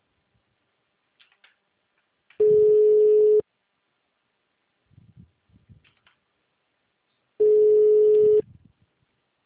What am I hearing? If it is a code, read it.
Ringback tone (Europe, 425 Hz), with speech.
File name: out- -2011- - - .45929